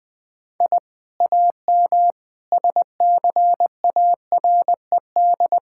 Morse code, heard as a keyed tone, about 20 wpm, 700 Hz. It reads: I AM SCARED
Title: Sentences from Top 1000 Words - Review Part 5 - 20wpm